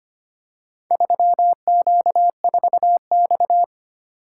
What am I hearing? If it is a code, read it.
Morse code, heard as a keyed tone, about 25 wpm, 700 Hz. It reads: 3Q4X
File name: practice / 25wpm / Four Letter-Number Combinations - 25wpm